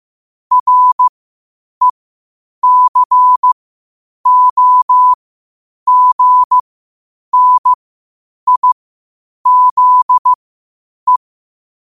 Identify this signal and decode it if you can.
Morse code, heard as a keyed tone, about 15 wpm, spaced as for 8 wpm, 1000 Hz. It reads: RECOGNIZE
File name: qotdfiles / 8wpm